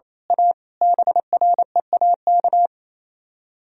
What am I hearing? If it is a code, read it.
Morse code, heard as a keyed tone, about 27 wpm, 700 Hz. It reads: A BREAK
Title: Speed-Racing - Sets of 5 Words 28wpm